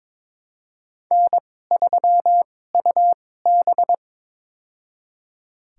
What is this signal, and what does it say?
Morse code, heard as a keyed tone, about 22 wpm, spaced as for 16 wpm, 700 Hz. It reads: N3UB
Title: USA Call Signs - 4 Characters - 2x Character Spacing - 22wpm